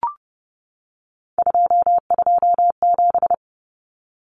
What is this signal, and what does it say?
Morse code, heard as a keyed tone, about 30 wpm, 700 Hz. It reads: 227
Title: Common Cut Numbers - 3-Digit 30wpm